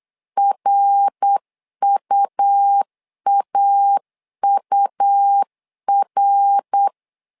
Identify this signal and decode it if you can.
Morse code, heard as a keyed tone, about 8 wpm, 800 Hz. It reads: RUAUR
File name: LEZIONI / 3 MWRO / 3 MWR 8wpm 2444